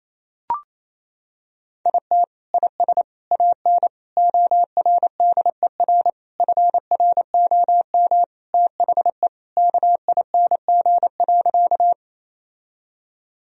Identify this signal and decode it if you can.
Morse code, heard as a keyed tone, about 28 wpm, 700 Hz. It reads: IT IS AN ORDER FROM THE KING.
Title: Sentences from Top 1000 Words - Review Part 2 - 28wpm